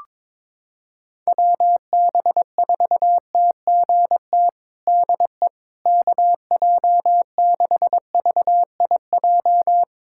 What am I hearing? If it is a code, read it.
Morse code, heard as a keyed tone, about 21 wpm, 700 Hz. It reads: WB4TGT DE KJ6VIJ
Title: USA Callsign DE Callsign - 6 Characters - 22wpm